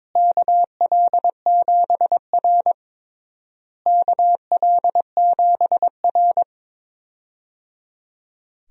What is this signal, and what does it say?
Morse code, heard as a keyed tone, about 21 wpm, 700 Hz. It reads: KL7R KL7R
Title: DXCC-100 Call Signs - 4 Characters - Repeated 3x - 22wpm